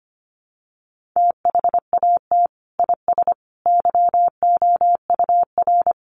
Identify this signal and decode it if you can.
Morse code, heard as a keyed tone, about 25 wpm, 700 Hz. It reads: THAT IS YOUR
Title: Sentences from Top 1200 Words 25wpm